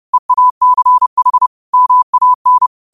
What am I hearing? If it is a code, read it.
Morse code, heard as a keyed tone, about 30 wpm, 1000 Hz. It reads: EACH MAN
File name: qotdfiles / 30wpm